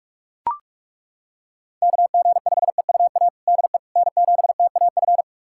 Morse code, heard as a keyed tone, about 41 wpm, 700 Hz. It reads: KG5EUA DE N7TAF
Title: USA Callsign DE Callsign 45wpm